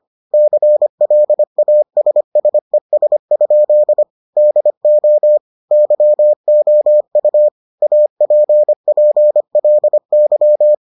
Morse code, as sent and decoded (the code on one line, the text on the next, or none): -.-. .-.. .- ... ... ... ..--.. / -.. --- / -.-- --- ..- / .- .--. .--. .-.. -.--
CLASSES? DO YOU APPLY